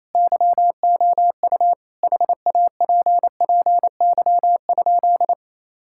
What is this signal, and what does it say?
Morse code, heard as a keyed tone, about 28 wpm, 700 Hz. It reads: YOU HAPPY?
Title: Sentences from Top 1000 Words - Review Part 5 - 28wpm